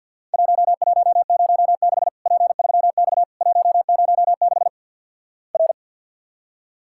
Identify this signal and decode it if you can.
Morse code, heard as a keyed tone, about 50 wpm, 700 Hz. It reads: 1106 P3X 106 R
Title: Worked All Europe DX Contest - QTC Exchange with Repeats - 50wpm